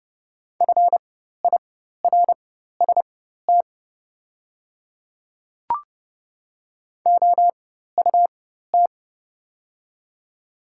Morse code, heard as a keyed tone, about 32 wpm, spaced as for 14 wpm, 700 Hz. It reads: FIRST OUT
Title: 4x Character Spacing - Top 100 Words - 30wpm